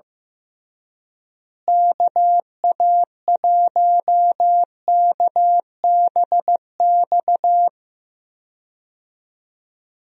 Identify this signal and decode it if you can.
Morse code, heard as a keyed tone, about 15 wpm, 700 Hz. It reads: KA1KBX
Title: USA Call Signs - 6 Characters - Repeated 2x - 15wpm